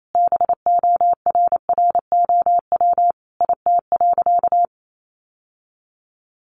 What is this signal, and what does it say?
Morse code, heard as a keyed tone, about 28 wpm, 700 Hz. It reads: BORROW IT.
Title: Sentences from Top 1500 Words 28wpm